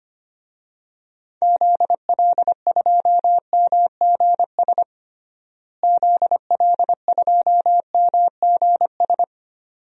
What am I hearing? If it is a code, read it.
Morse code, heard as a keyed tone, about 25 wpm, 700 Hz. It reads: ZL2MGS ZL2MGS
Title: DXCC-100 Call Signs - 6 Characters - Repeated 2x - 25wpm